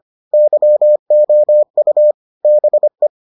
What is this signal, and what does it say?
Morse code, heard as a keyed tone, about 25 wpm, 600 Hz. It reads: YOU BE